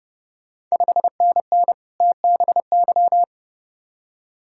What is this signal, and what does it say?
Morse code, heard as a keyed tone, about 30 wpm, 700 Hz. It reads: 5NN TBY
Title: Ontario QSO Party Exchange 30wpm